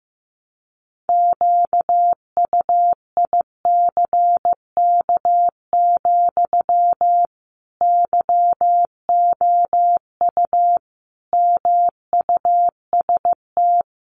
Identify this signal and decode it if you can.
Morse code, heard as a keyed tone, about 15 wpm, 700 Hz. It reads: QUICK, YOU MUST
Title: Sentences from Top 500 Words 15wpm